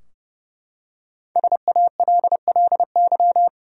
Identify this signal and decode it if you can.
Morse code, heard as a keyed tone, about 30 wpm, 700 Hz. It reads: SALLY
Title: USA Common Names - Max 5 Letters - 30wpm